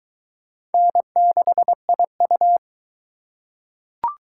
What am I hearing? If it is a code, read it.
Morse code, heard as a keyed tone, about 21 wpm, 700 Hz. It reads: N6IU